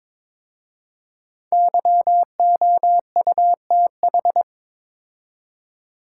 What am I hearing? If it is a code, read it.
Morse code, heard as a keyed tone, about 22 wpm, 700 Hz. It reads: YOUTH
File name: 5 Letter Words - First 3 Letters Given - 22wpm